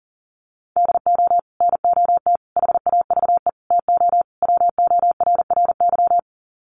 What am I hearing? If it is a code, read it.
Morse code, heard as a keyed tone, about 40 wpm, 700 Hz. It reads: DO NOT HAVE TO WORRY
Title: Sets of 5 Words 40wpm